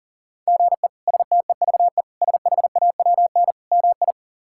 Morse code, heard as a keyed tone, about 40 wpm, 700 Hz. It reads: GE STEVE SHAWN MI
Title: K1USN SST QSO Practice - 40wpm